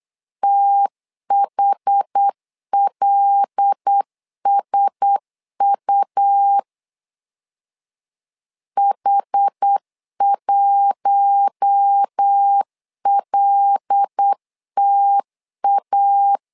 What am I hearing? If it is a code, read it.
Morse code, heard as a keyed tone, about 8 wpm, 800 Hz. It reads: THLSU H1LTA
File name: LEZIONI / 5 LBVC / 5 L 8wpm 2831